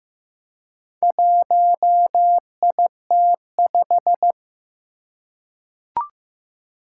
Morse code, heard as a keyed tone, about 15 wpm, 700 Hz. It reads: 1IT5
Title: Four Challenging Letter-Number Combinations 15wpm